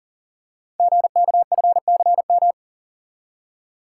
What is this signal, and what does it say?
Morse code, heard as a keyed tone, about 42 wpm, 700 Hz. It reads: GKFCM
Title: Code Groups - 5 Letters - 40wpm